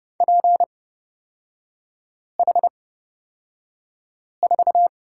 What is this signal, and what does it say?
Morse code, heard as a keyed tone, about 29 wpm, spaced as for 5 wpm, 690 Hz. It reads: PH4
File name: Lesson 38e - Letter q - Three Characters 30wpm